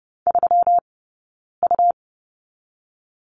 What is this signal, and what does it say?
Morse code, heard as a keyed tone, about 30 wpm, 700 Hz. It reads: 3 U
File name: Lesson 20d - Number 3 - Two Characters 30wpm